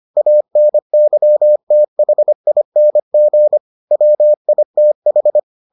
Morse code, heard as a keyed tone, about 25 wpm, 600 Hz. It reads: ANYTHING WITH